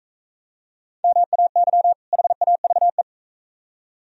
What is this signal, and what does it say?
Morse code, heard as a keyed tone, about 42 wpm, 700 Hz. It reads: MAY HAVE